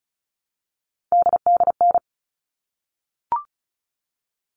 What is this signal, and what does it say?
Morse code, heard as a keyed tone, about 34 wpm, 700 Hz. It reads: DDN
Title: All Cut Numbers - 3-Digit 35wpm